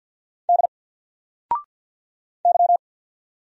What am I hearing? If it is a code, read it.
Morse code, heard as a keyed tone, about 50 wpm, 700 Hz. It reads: D Y
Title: Single Letter-Number 50wpm